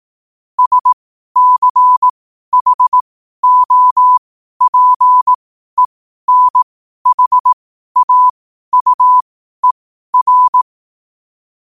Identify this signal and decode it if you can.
Morse code, heard as a keyed tone, about 18 wpm, spaced as for 12 wpm, 1000 Hz. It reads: SCHOPENHAUER
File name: qotdfiles / 12wpm